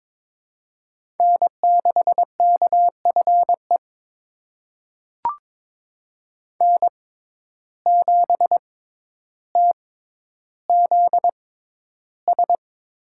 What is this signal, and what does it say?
Morse code, heard as a keyed tone, about 22 wpm, spaced as for 11 wpm, 700 Hz. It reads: N6KFE N7TZS
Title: USA Call Signs - 5 Characters - 6x Character Spacing - 22wpm